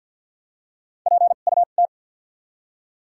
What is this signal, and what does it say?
Morse code, heard as a keyed tone, about 48 wpm, spaced as for 35 wpm, 700 Hz. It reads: PUT
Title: 2x Character Spacing - Top 300 Words - 50wpm